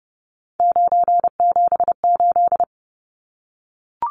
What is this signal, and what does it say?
Morse code, heard as a keyed tone, about 30 wpm, 700 Hz. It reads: 978
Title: Numbers - 3-Digit 30wpm